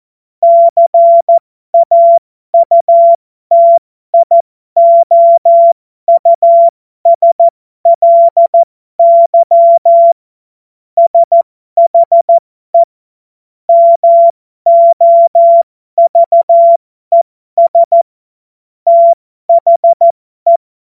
Morse code, heard as a keyed tone, about 14 wpm, spaced as for 12 wpm, 680 Hz. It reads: CAUTIOUSLY SHE MOVES THE
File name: Morsekurs / Witze